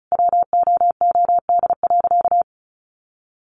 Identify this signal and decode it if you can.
Morse code, heard as a keyed tone, about 34 wpm, 700 Hz. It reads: WOOD.
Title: Sentences from Top 1500 Words 35wpm